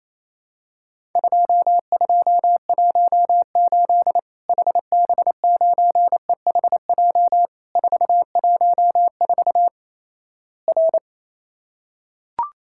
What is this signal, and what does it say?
Morse code, heard as a keyed tone, about 28 wpm, 700 Hz. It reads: 2218 HB9EHJ 414 R
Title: Worked All Europe DX Contest - QTC Exchange - 28wpm